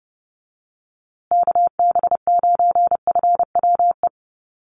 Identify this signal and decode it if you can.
Morse code, heard as a keyed tone, about 30 wpm, 700 Hz. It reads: KB9FWE
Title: USA Call Signs Repeated 2x 30wpm